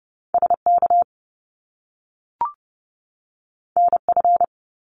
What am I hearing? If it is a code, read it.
Morse code, heard as a keyed tone, about 30 wpm, 700 Hz. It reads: SK NF
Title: Top Two Letter Combinations in Words 30wpm